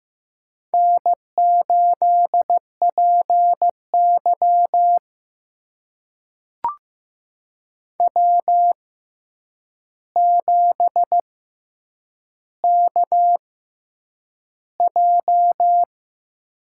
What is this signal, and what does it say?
Morse code, heard as a keyed tone, about 15 wpm, spaced as for 8 wpm, 700 Hz. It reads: N8PY W7KJ
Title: USA Call Signs - 4 Characters - 6x Character Spacing - 15wpm